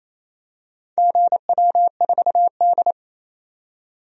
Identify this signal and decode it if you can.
Morse code, heard as a keyed tone, about 27 wpm, 700 Hz. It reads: GW4D
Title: DXCC-100 Call Signs - 4 Characters - 28wpm